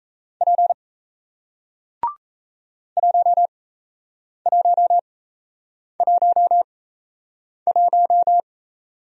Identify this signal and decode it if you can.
Morse code, heard as a keyed tone, about 38 wpm, spaced as for 8 wpm, 700 Hz. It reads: P 1111